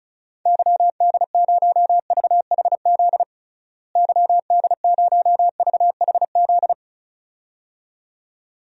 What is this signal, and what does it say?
Morse code, heard as a keyed tone, about 36 wpm, 700 Hz. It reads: YD0VHZ YD0VHZ
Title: DXCC-100 Call Signs - 6 Characters - Repeated 2x - 35wpm